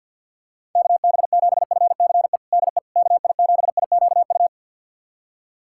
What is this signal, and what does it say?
Morse code, heard as a keyed tone, about 52 wpm, 700 Hz. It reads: KB7FCE DE KI7IQU
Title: USA Callsign DE Callsign - 6 Characters - 50wpm